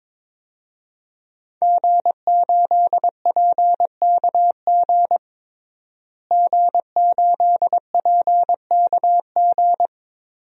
Morse code, heard as a keyed tone, about 22 wpm, 700 Hz. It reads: G8PKG G8PKG
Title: DXCC-100 Call Signs - 5 Characters - Repeated 3x - 22wpm